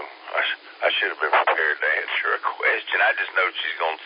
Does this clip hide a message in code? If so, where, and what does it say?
Yes: Emergency Alert System (SAME) tones from 0.3 to 3.3 s, heard as harsh buzzing bursts, with speech.